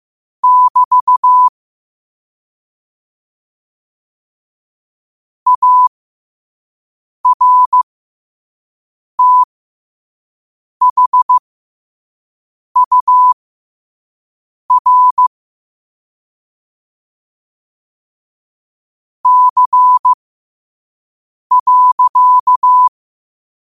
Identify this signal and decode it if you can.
Morse code, heard as a keyed tone, about 15 wpm, spaced as for 5 wpm, 1000 Hz. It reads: <BT> ARTHUR C.